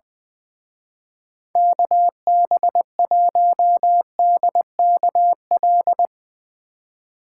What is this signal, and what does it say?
Morse code, heard as a keyed tone, about 22 wpm, 700 Hz. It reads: KB1DKL
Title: Speed Racing - USA 2x3 Call Signs 20wpm